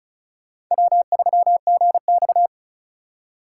Morse code, heard as a keyed tone, about 36 wpm, 700 Hz. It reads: W3GX
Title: Speed Racing - USA 1x2 Call Signs 30wpm